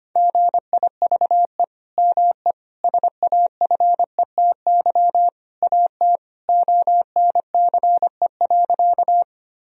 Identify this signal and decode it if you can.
Morse code, heard as a keyed tone, about 25 wpm, 700 Hz. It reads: GIVE ME SAFETY AT ONCE.
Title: Sentences from Top 1000 Words - Review Part 4 - 25wpm